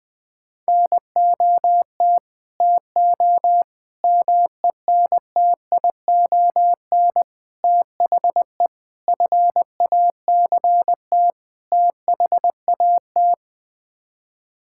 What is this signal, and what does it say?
Morse code, heard as a keyed tone, about 19 wpm, 700 Hz. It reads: NOT TO MENTION THE FACT THAT
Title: Sets of 6 Words - 20wpm